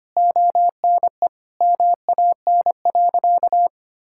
Morse code, heard as a keyed tone, about 25 wpm, 700 Hz. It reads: ONE MAN.
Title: Sentences from Top 1000 Words - Review Part 6 - 25wpm